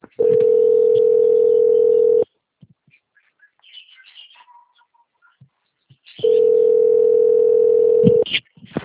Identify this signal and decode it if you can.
Ringback tone (North America), 440 + 480 Hz.